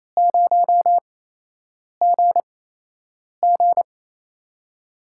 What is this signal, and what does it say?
Morse code, heard as a keyed tone, about 28 wpm, spaced as for 8 wpm, 700 Hz. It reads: 0GG
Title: USA Call Signs - 4 Characters - 8x Character Spacing - 28wpm